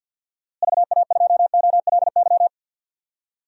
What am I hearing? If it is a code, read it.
Morse code, heard as a keyed tone, about 53 wpm, 700 Hz. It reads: VA2OLY
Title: DXCC-100 Call Signs - 6 Characters - Repeated 2x - 50wpm